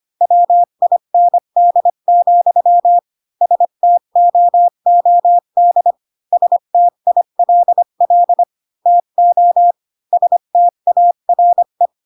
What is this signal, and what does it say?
Morse code, heard as a keyed tone, about 25 wpm, 700 Hz. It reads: WIND, STOOD STILL TO STARE
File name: Pinocchio / Pinocchio ch3 20 25 ik1hge